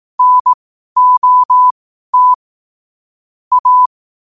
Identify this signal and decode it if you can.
Morse code, heard as a keyed tone, about 18 wpm, spaced as for 12 wpm, 1000 Hz. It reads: NOT A
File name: qotdfiles / 12wpm